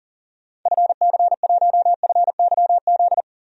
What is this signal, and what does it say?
Morse code, heard as a keyed tone, about 40 wpm, 700 Hz. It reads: FC1FYZ
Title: DXCC-100 Call Signs - 6 Characters - 40wpm